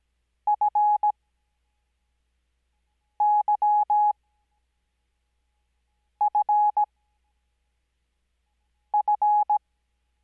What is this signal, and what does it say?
Morse code, heard as a keyed tone, about 17 wpm, spaced as for 4 wpm, 840 Hz. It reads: FYFF
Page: Farnsworth Spacing: 15WPM Character speed Spaced at 5WPM.